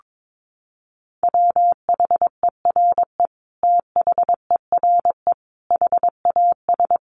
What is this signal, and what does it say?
Morse code, heard as a keyed tone, about 22 wpm, 700 Hz. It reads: WHERE THERE HAS
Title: Sentences from Top 1400 Words 22wpm